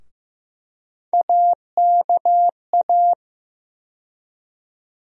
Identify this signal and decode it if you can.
Morse code, heard as a keyed tone, about 15 wpm, 700 Hz. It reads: AKA
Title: USA Call Sign Postfixes - 3 Letters 15wpm